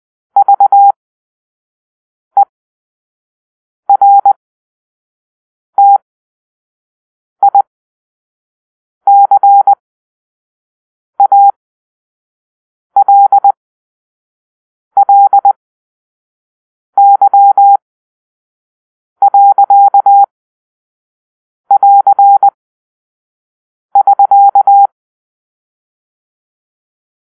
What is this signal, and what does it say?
Morse code, heard as a keyed tone, about 20 wpm, spaced as for 5 wpm, 800 Hz. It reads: VERTICALLY.<AR><SK>
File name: arrl / Grp-5 Drill-2